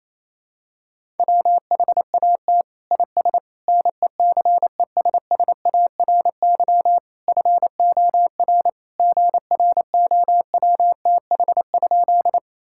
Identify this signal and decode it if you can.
Morse code, heard as a keyed tone, about 28 wpm, 700 Hz. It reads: WHAT IS NECESSARY FOR GROWTH?
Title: Sentences from Top 1000 Words - Review Part 5 - 28wpm